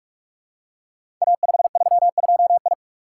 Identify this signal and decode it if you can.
Morse code, heard as a keyed tone, about 44 wpm, 700 Hz. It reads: A532I
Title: Code Groups - 5 Letter:Numbers - 45wpm